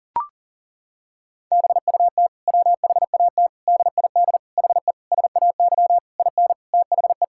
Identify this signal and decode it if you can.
Morse code, heard as a keyed tone, about 37 wpm, 700 Hz. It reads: BUT WHAT DID HE SAY IN THE
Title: Sentences from Top 1000 Words - Review Part 6 - 40wpm